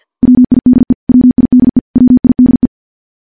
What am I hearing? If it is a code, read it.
Morse code, heard as a keyed tone, about 50 wpm, 250 Hz. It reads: WIDE WIDE WIDE
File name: morse / morse-training-50WPM